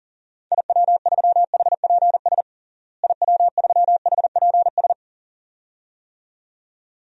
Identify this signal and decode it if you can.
Morse code, heard as a keyed tone, about 40 wpm, 700 Hz. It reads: IW3HPS IW3HPS
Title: DXCC-100 Call Signs - 6 Characters - Repeated 2x - 40wpm